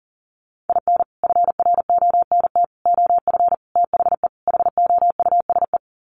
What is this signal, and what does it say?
Morse code, heard as a keyed tone, about 37 wpm, 700 Hz. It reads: IN FRONT OF THE HOUSE
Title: Sets of 5 Words Encore2 40wpm